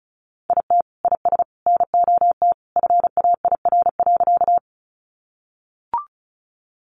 Morse code, heard as a keyed tone, about 34 wpm, 700 Hz. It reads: IT IS NOT FAIR.
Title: Sentences from Top 1300 Words 35wpm